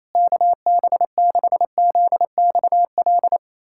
Morse code, heard as a keyed tone, about 28 wpm, 700 Hz. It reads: KB6ZXL
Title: USA Callsign DE Callsign - 6 Characters - 28wpm